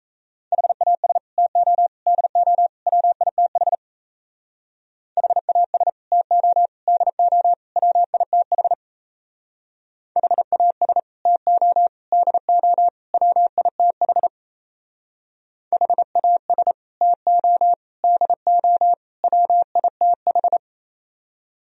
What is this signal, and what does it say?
Morse code, heard as a keyed tone, about 34 wpm, 700 Hz. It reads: HAS TO DO WITH HAS TO DO WITH HAS TO DO WITH HAS TO DO WITH
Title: Speed-Racing - Sets of 4 Words 28wpm